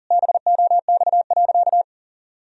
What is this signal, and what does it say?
Morse code, heard as a keyed tone, about 40 wpm, 700 Hz. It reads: BOX.
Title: Sentences from Top 1000 Words - Review Part 2 - 40wpm